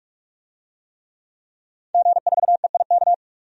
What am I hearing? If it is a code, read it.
Morse code, heard as a keyed tone, about 41 wpm, 700 Hz. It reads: G4EIK
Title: DXCC-100 Call Signs - 5 Characters - Repeated 2x - 45wpm